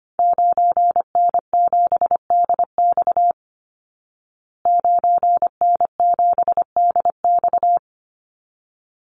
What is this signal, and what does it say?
Morse code, heard as a keyed tone, about 25 wpm, 700 Hz. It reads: 9N7DX 9N7DX
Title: Call Signs Repeated 3x - 25wpm